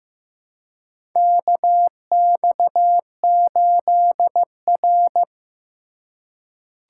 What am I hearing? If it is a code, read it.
Morse code, heard as a keyed tone, about 15 wpm, 700 Hz. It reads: KX8R
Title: USA Call Signs - 4 Characters - 2x Character Spacing - 15wpm